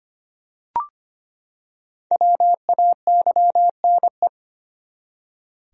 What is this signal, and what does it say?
Morse code, heard as a keyed tone, about 25 wpm, 700 Hz. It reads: WAYNE
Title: USA Common Names - Max 5 Letters - 25wpm